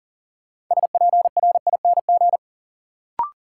Morse code, heard as a keyed tone, about 40 wpm, 700 Hz. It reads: SPRING
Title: Sentences from Top 1200 Words - New Word Drill - 40wpm